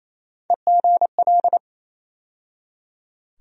Morse code, heard as a keyed tone, about 28 wpm, 700 Hz. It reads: EGL
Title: USA Call Sign Postfixes - 3 Letters 28wpm